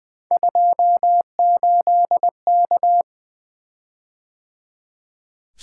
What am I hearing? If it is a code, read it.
Morse code, heard as a keyed tone, about 20 wpm, 700 Hz. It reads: 28K
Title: Three Letter-Number Combinations - 20wpm